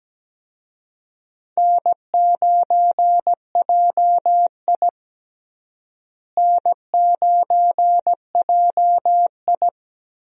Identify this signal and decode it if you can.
Morse code, heard as a keyed tone, about 17 wpm, 700 Hz. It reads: N9JI N9JI
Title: USA Call Signs - 4 Characters - Repeated 3x - 17wpm